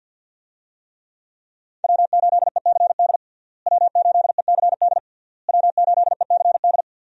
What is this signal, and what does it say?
Morse code, heard as a keyed tone, about 47 wpm, 700 Hz. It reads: W8ECD W8ECD W8ECD
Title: USA Call Signs - 5 Characters - Repeated 3x - 50wpm